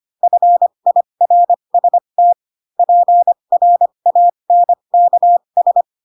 Morse code, heard as a keyed tone, about 25 wpm, spaced as for 22 wpm, 700 Hz. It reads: FIRST PRANKS
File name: Pinocchio / Pinocchio ch3 18 25 ik1hge